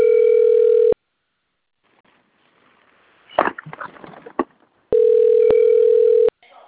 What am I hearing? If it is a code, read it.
Ringback tone (North America), 440 + 480 Hz.
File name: out- - - - - .2976